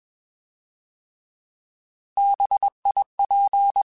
Morse code, heard as a keyed tone, about 21 wpm, 800 Hz. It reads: BIP
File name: Alphabetical / Alphabetical mp3s / Bs @20